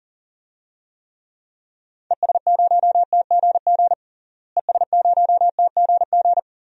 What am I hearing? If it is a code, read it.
Morse code, heard as a keyed tone, about 40 wpm, 700 Hz. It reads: ES0TGG ES0TGG
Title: DXCC-100 Call Signs - 6 Characters - Repeated 3x - 40wpm